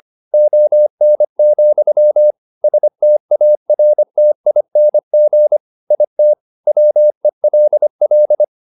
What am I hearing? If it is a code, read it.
Morse code, heard as a keyed tone, about 25 wpm, 600 Hz. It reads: ON, STARTING IT WELL